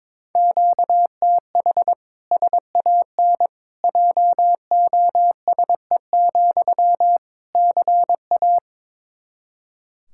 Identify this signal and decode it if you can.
Morse code, heard as a keyed tone, about 22 wpm, 700 Hz. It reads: QTH SAN JOSE, CA
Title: Practice QSO - Exchange 1 - Beginning - 22wpm